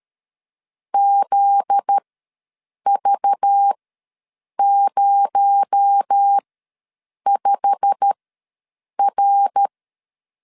Morse code, heard as a keyed tone, about 13 wpm, spaced as for 7 wpm, 800 Hz. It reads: ZV05R
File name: LEZIONI / 9 28 ws 12wpm 3354